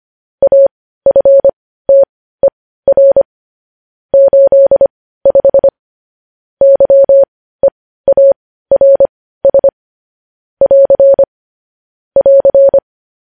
Morse code, heard as a keyed tone, about 25 wpm, spaced as for 15 wpm, 550 Hz. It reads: AFTER 85 YEARS <AR> <AR>
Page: news-headlines-in-morse-code-at-15-wpm